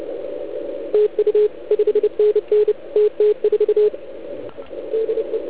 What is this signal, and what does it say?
Morse code, heard as a keyed tone, about 28 wpm, 440 Hz. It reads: TU 5NN TT4 DE